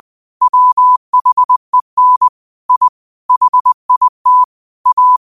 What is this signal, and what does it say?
Morse code, heard as a keyed tone, about 20 wpm, 1000 Hz. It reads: WHEN I HIT A
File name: qotdfiles / 20wpm